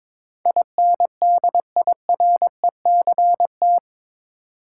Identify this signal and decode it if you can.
Morse code, heard as a keyed tone, about 22 wpm, 700 Hz. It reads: INDIRECT
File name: practice / 22wpm / Eight Letter Words - 22wpm